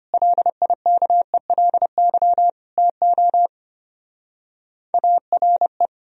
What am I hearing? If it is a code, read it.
Morse code, heard as a keyed tone, about 31 wpm, 700 Hz. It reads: LIKELY TO ARE